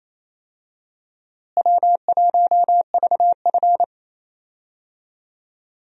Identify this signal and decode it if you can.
Morse code, heard as a keyed tone, about 28 wpm, 700 Hz. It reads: W1VF
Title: USA Call Signs - 4 Characters - Repeated 3x - 28wpm